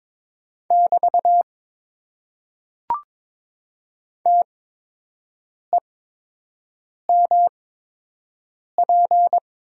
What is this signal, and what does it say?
Morse code, heard as a keyed tone, about 22 wpm, spaced as for 6 wpm, 700 Hz. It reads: <BT> TEMP
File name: Top 90 QSO Elements - 8x Character Spacing - 22wpm